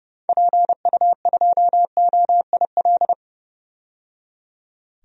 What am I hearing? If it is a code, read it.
Morse code, heard as a keyed tone, about 30 wpm, 700 Hz. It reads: PU2OIL
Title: DXCC-100 Call Signs - 6 Characters - 30wpm